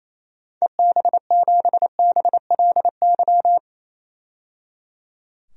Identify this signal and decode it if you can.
Morse code, heard as a keyed tone, about 28 wpm, 700 Hz. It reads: EB7BLY
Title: DXCC-100 Call Signs - 6 Characters - 28wpm